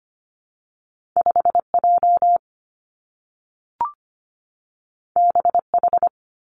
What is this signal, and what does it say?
Morse code, heard as a keyed tone, about 25 wpm, 700 Hz. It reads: 5 J B H
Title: Two Challenging Letter-Number Combinations 25wpm